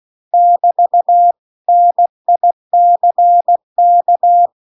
Morse code, heard as a keyed tone, about 16 wpm, 700 Hz. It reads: <BT> NICK